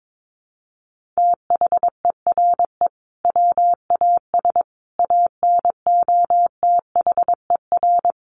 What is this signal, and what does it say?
Morse code, heard as a keyed tone, about 22 wpm, 700 Hz. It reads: THERE WAS ANOTHER
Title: Sentences from Top 1300 Words 22wpm